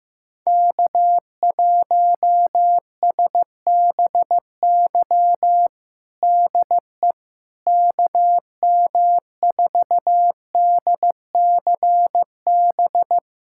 Morse code, heard as a keyed tone, about 15 wpm, 700 Hz. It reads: K1SBY DE KM4DCB
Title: USA Callsign DE Callsign 15wpm